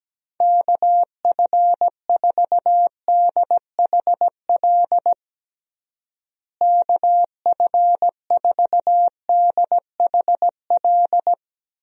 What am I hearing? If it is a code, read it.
Morse code, heard as a keyed tone, about 17 wpm, 700 Hz. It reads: KF4DHL KF4DHL
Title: USA Call Signs - 6 Characters - Repeated 2x - 17wpm